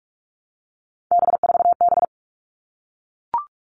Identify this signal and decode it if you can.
Morse code, heard as a keyed tone, about 44 wpm, 700 Hz. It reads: B4B
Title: All Cut Numbers - 3-Digit 45wpm